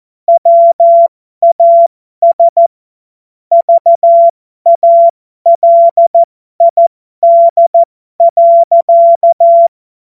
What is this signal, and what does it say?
Morse code, heard as a keyed tone, about 14 wpm, spaced as for 12 wpm, 680 Hz. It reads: WAS VALID.
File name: Morsekurs / Witze